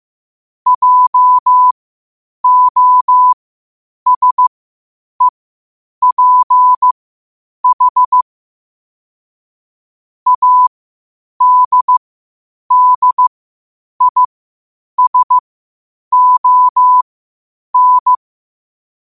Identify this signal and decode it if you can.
Morse code, heard as a keyed tone, about 15 wpm, spaced as for 8 wpm, 1000 Hz. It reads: JOSEPH ADDISON
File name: qotdfiles / 8wpm